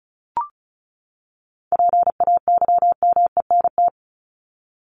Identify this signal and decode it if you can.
Morse code, heard as a keyed tone, about 34 wpm, 700 Hz. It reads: PAYMENT
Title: Top 1001-1250 Words 35wpm